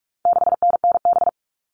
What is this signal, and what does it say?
Morse code, heard as a keyed tone, about 44 wpm, 700 Hz. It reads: 6NNB
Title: All Cut Numbers - 4-Digit 45wpm